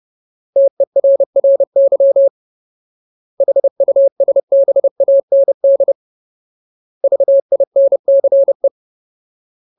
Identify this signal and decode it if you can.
Morse code, heard as a keyed tone, about 30 wpm, 550 Hz. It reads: TERRY HUSBAND VINCE
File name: Ultimate Word List / 30WPM Ultimate